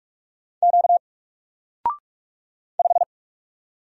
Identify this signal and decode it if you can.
Morse code, heard as a keyed tone, about 44 wpm, 700 Hz. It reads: Q 5
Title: Single Character (Letter:Number:Punctuation:Prosign) Alt 45wpm